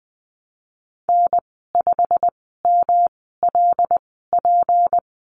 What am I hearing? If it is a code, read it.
Morse code, heard as a keyed tone, about 20 wpm, spaced as for 14 wpm, 700 Hz. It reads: N5MLP
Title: 173 Lesson 28g - Number 9 - Call Signs 20wpm